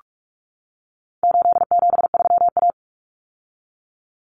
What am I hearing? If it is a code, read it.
Morse code, heard as a keyed tone, about 44 wpm, 700 Hz. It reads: 873A
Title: Common Cut Numbers - 4-Digit 45wpm